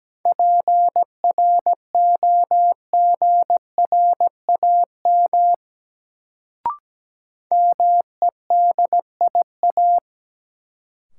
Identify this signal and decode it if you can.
Morse code, heard as a keyed tone, about 17 wpm, 700 Hz. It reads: PROGRAM MEDIA
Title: Sentences from Top 1000 Words - Review Part 4 - New Word Drill - 17wpm